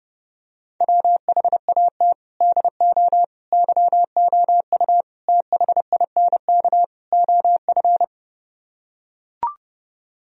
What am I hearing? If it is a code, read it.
Morse code, heard as a keyed tone, about 32 wpm, 700 Hz. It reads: WHAT DO YOU THINK OF